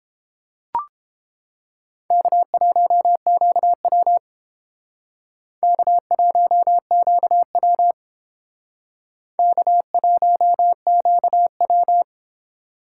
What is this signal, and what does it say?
Morse code, heard as a keyed tone, about 31 wpm, 700 Hz. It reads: K1QW K1QW K1QW